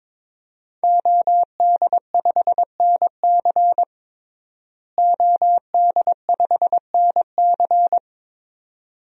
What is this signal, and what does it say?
Morse code, heard as a keyed tone, about 22 wpm, 700 Hz. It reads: OD5NC OD5NC
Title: DXCC-100 Call Signs - 5 Characters - Repeated 2x - 22wpm